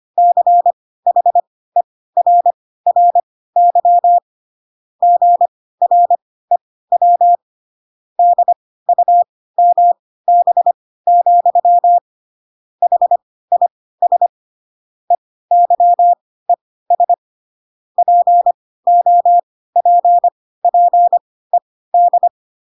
Morse code, heard as a keyed tone, about 25 wpm, spaced as for 16 wpm, 700 Hz. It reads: CHERRY GREW DUMB, HIS EYES POPPED
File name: Pinocchio / Pinocchio ch1 10 25 ik1hge